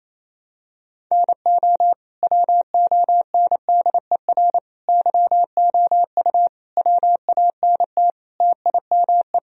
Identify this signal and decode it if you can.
Morse code, heard as a keyed tone, about 28 wpm, 700 Hz. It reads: NO WONDER YOU WANT TIME
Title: Sentences from Top 1000 Words - Review Part 3 - 28wpm